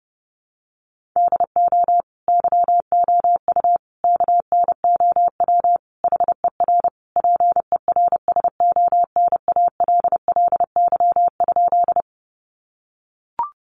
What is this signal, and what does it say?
Morse code, heard as a keyed tone, about 30 wpm, 700 Hz. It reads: DO YOU KNOW HER PERSONALLY?
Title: Sentences from Top 1400 Words 30wpm